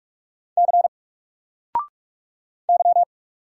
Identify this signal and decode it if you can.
Morse code, heard as a keyed tone, about 44 wpm, 700 Hz. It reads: C Y
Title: Single Character (Letter:Number:Punctuation:Prosign) 45wpm